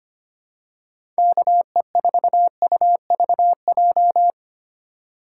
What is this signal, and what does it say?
Morse code, heard as a keyed tone, about 25 wpm, 700 Hz. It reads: KE4UVJ
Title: USA Call Signs - 6 Characters - 2x Character Spacing - 25wpm